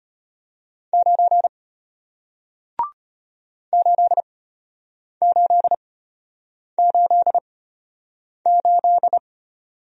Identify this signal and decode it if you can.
Morse code, heard as a keyed tone, about 34 wpm, spaced as for 8 wpm, 700 Hz. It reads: 9 8888